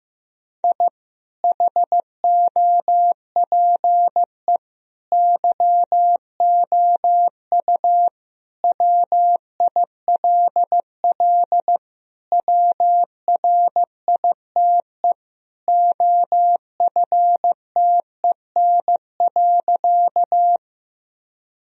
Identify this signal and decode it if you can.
Morse code, heard as a keyed tone, about 16 wpm, 700 Hz. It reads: I HOPE YOU WILL WRITE OFTEN.
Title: Sentences from Top 1000 Words - Review Part 1 - 15wpm